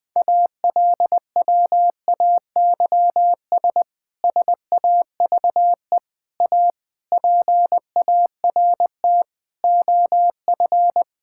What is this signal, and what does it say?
Morse code, heard as a keyed tone, about 20 wpm, 700 Hz. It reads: ALWAYS SAVE A PART OF
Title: Difficult Sentences with Hints - 20wpm